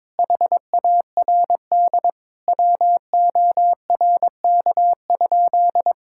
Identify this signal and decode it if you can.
Morse code, heard as a keyed tone, about 22 wpm, 700 Hz. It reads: HARD WORK?
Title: Sentences from Top 1000 Words - Review Part 6 - 22wpm